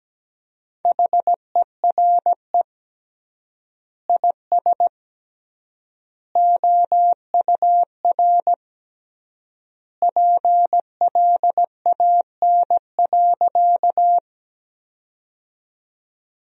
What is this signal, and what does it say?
Morse code, heard as a keyed tone, about 17 wpm, 700 Hz. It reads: HERE IS OUR PLAN.
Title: 3x Word-Spacing - Sentences from Top 500 Words - 17wpm